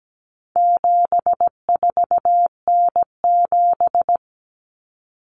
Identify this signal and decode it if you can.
Morse code, heard as a keyed tone, about 17 wpm, 700 Hz. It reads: 74N7
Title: Common Cut Numbers - 4-Digit 17wpm